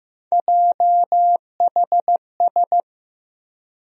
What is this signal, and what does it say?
Morse code, heard as a keyed tone, about 15 wpm, 700 Hz. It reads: JHS
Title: USA Call Sign Postfixes - 3 Letters 15wpm